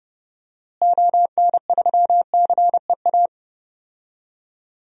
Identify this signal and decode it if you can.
Morse code, heard as a keyed tone, about 30 wpm, 700 Hz. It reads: ON3CEA
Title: DXCC-100 Call Signs - 6 Characters - Repeated 3x - 30wpm